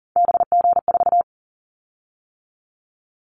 Morse code, heard as a keyed tone, about 38 wpm, 700 Hz. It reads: BG4
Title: Callsign Prefixes for Top 100 DXCC - 40wpm